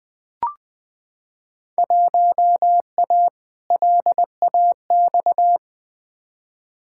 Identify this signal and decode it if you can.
Morse code, heard as a keyed tone, about 20 wpm, 700 Hz. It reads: 1A LAX
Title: Field Day Exchanges Encore - 20wpm